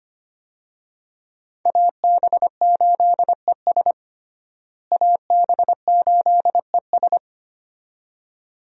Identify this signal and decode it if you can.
Morse code, heard as a keyed tone, about 25 wpm, 700 Hz. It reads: AB8ES AB8ES
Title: USA Call Signs - 5 Characters - Repeated 2x - 25wpm